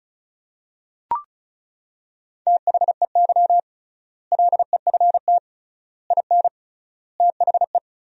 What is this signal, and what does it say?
Morse code, heard as a keyed tone, about 34 wpm, 700 Hz. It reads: THEY LEFT IN THE
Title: 3x Word-Spacing - Sentences from Top 400 Words - 35wpm